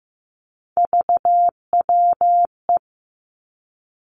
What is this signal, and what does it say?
Morse code, heard as a keyed tone, about 15 wpm, 700 Hz. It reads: VWE
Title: Three Challenging Letter-Number Combinations 15wpm